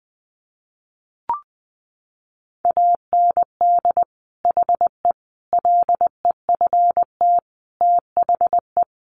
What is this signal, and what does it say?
Morse code, heard as a keyed tone, about 20 wpm, 700 Hz. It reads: AND HE LEFT THE
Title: Sentences from Top 300 Words 20wpm